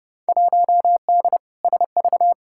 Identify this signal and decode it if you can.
Morse code, heard as a keyed tone, about 30 wpm, 700 Hz. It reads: 1D SV
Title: Field Day Exchanges - 30wpm